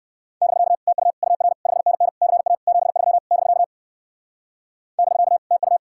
Morse code, heard as a keyed tone, about 54 wpm, 700 Hz. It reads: <BK> TU UR 5NN CA 73 <BK> <BK> TU
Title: Speed-Racing Common QSO Phrases - 45wpm